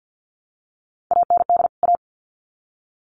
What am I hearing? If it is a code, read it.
Morse code, heard as a keyed tone, about 50 wpm, 700 Hz. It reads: AND A
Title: Sets of 2 Words 50wpm